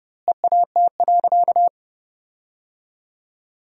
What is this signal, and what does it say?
Morse code, heard as a keyed tone, about 32 wpm, 700 Hz. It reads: EAT.
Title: 2x Word-Spacing - Sentences from Top 300 Words - 30wpm